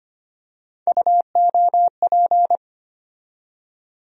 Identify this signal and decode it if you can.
Morse code, heard as a keyed tone, about 25 wpm, 700 Hz. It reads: UOP
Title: USA Call Sign Postfixes - 3 Letters 25wpm